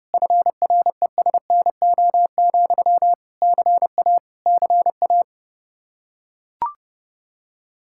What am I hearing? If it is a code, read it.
Morse code, heard as a keyed tone, about 30 wpm, 700 Hz. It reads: FRESNO, CA CA
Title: Practice QSO - Exchange 1 - Beginning - 30wpm